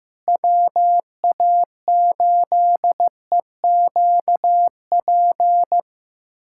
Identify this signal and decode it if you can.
Morse code, heard as a keyed tone, about 16 wpm, 700 Hz. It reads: WA8EQP
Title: USA Callsign DE Callsign - 6 Characters - 15wpm